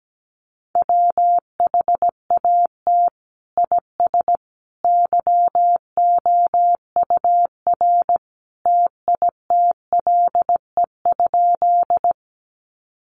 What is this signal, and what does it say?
Morse code, heard as a keyed tone, about 17 wpm, 700 Hz. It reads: WHAT IS YOUR TITLE?
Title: Sentences from Top 1200 Words 17wpm